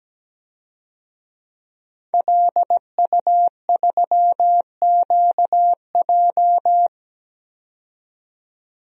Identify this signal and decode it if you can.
Morse code, heard as a keyed tone, about 17 wpm, 700 Hz. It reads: LU3QJ
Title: DXCC-100 Call Signs - 5 Characters - 17wpm